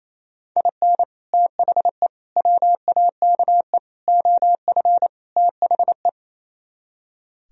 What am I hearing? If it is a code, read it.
Morse code, heard as a keyed tone, about 26 wpm, 700 Hz. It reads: IN THE WAKE OF THE
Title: Speed-Racing - Sets of 5 Words 28wpm